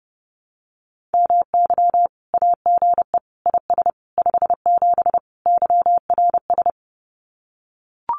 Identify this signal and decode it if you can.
Morse code, heard as a keyed tone, about 30 wpm, 700 Hz. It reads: MY AGE IS 57 YRS
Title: Common QSO Phrases Encore 30wpm